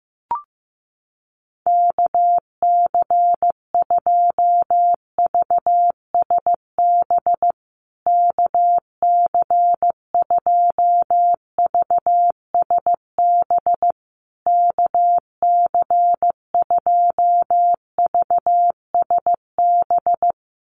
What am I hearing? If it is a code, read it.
Morse code, heard as a keyed tone, about 15 wpm, 700 Hz. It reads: KC2VSB KC2VSB KC2VSB
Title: USA Call Signs Repeated 3x 15wpm